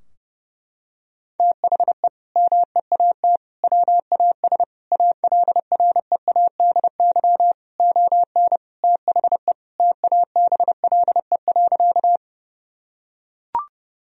Morse code, heard as a keyed tone, about 32 wpm, 700 Hz. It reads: THE MEAT WAS ALREADY ON THE TABLE.
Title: Sentences from Top 1000 Words - Review Part 3 - 30wpm